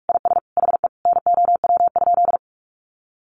Morse code, heard as a keyed tone, about 44 wpm, 700 Hz. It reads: IS HE NOW?
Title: Sentences from Top 200 Words 45wpm